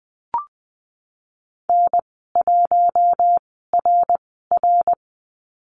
Morse code, heard as a keyed tone, about 20 wpm, 700 Hz. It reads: N 1 R R
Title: 082 Lesson 14g - Number 2 - Call Signs 20wpm